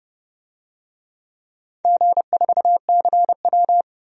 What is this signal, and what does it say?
Morse code, heard as a keyed tone, about 32 wpm, 700 Hz. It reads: G4CW
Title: DXCC-100 Call Signs - 4 Characters - 30wpm